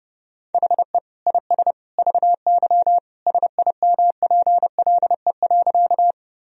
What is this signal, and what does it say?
Morse code, heard as a keyed tone, about 32 wpm, 700 Hz. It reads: HE IS VY SIMPLE.
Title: Sentences from Top 400 Words and Abbreviations 30wpm